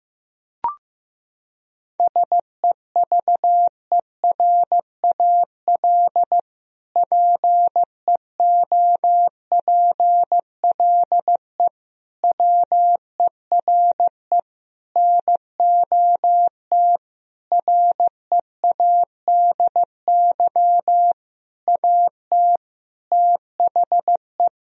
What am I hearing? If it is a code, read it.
Morse code, heard as a keyed tone, about 16 wpm, 700 Hz. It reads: SEVERAL PEOPLE WERE NOT READY AT THE
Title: Sentences from Top 1000 Words - Review Part 2 - 15wpm